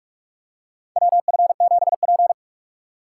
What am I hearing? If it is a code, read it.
Morse code, heard as a keyed tone, about 46 wpm, 700 Hz. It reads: WF7P
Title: USA Call Signs - 4x Character Spacing - 45wpm